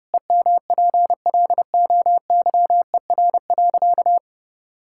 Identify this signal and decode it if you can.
Morse code, heard as a keyed tone, about 30 wpm, 700 Hz. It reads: EMPLOYER.
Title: Pangram Challenge v2 - 30wpm